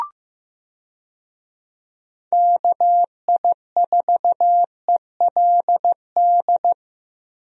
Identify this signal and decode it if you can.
Morse code, heard as a keyed tone, about 15 wpm, 700 Hz. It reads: KI4ELD
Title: USA Call Signs - 6 Characters - Repeated 3x - 15wpm